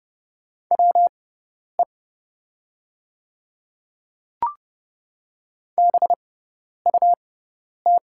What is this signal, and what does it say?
Morse code, heard as a keyed tone, about 30 wpm, spaced as for 10 wpm, 690 Hz. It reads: WE BUT
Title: 6x Character Spacing - Top 100 Words - 30wpm